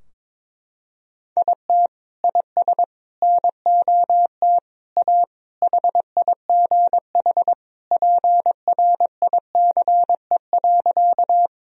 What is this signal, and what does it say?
Morse code, heard as a keyed tone, about 22 wpm, 700 Hz. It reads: IT IS NOT A HIGH PRICE.
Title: Sentences from Top 1000 Words - Review Part 4 - 22wpm